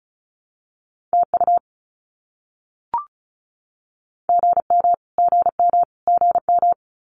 Morse code, heard as a keyed tone, about 34 wpm, 700 Hz. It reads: TU GM GM GM
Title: Abbreviations Repeated 3x - 35wpm